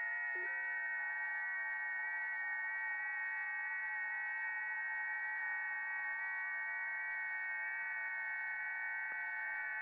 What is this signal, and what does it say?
FT8.